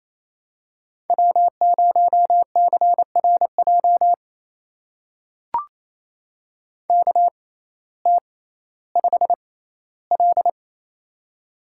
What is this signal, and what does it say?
Morse code, heard as a keyed tone, about 29 wpm, spaced as for 16 wpm, 700 Hz. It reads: W0CRJ KT5L
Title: USA Call Signs - 6x Character Spacing - 28wpm